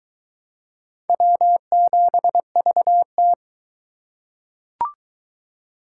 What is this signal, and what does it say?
Morse code, heard as a keyed tone, about 21 wpm, 700 Hz. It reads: W7VT